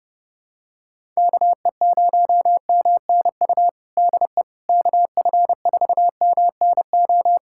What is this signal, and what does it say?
Morse code, heard as a keyed tone, about 32 wpm, 700 Hz. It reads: KE0MNU DE KF4MNO
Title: USA Callsign DE Callsign - 6 Characters - 30wpm